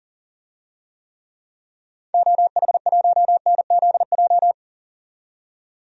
Morse code, heard as a keyed tone, about 40 wpm, 700 Hz. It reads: OH1NZJ
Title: DXCC-100 Call Signs - 6 Characters - 40wpm